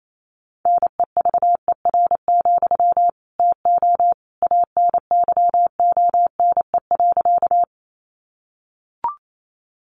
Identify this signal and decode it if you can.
Morse code, heard as a keyed tone, about 28 wpm, 700 Hz. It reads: NEVER, TO ANYONE.
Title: Sentences from Top 900 Words 28wpm